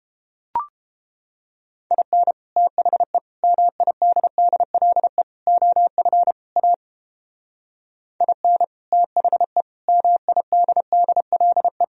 Morse code, heard as a keyed tone, about 29 wpm, 700 Hz. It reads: IN THE MIDDLE OF A IN THE MIDDLE